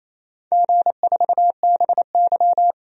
Morse code, heard as a keyed tone, about 27 wpm, 690 Hz. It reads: G4BY
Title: DXCC-100 Call Signs - 4 Characters - 28wpm